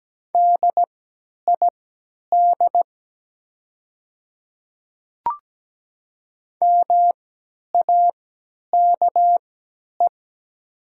Morse code, heard as a keyed tone, about 17 wpm, spaced as for 10 wpm, 700 Hz. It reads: DID MAKE
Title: 3x Character Spacing - Top 200 Words - 17wpm